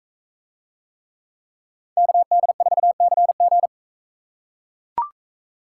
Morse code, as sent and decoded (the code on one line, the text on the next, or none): -.- -.. ....- -.-. --.
KD4CG